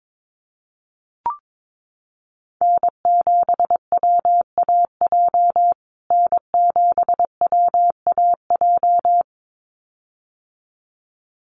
Morse code, heard as a keyed tone, about 22 wpm, 700 Hz. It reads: N7WAJ N7WAJ
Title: USA Call Signs Repeated 2x 22wpm